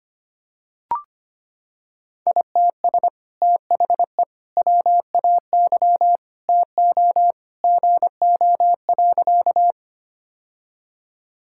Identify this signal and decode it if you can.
Morse code, heard as a keyed tone, about 25 wpm, 700 Hz. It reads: ITS THE WAY TO GO.
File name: practice / 25wpm / Sentences with 2-Word Phrases - 25wpm